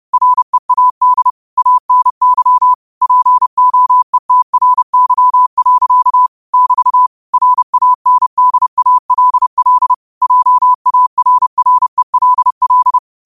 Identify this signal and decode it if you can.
Morse code, heard as a keyed tone, about 30 wpm, 1000 Hz. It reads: READ ANY POETRY. <BT> RANDALL JARRELL
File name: qotdfiles / 30wpm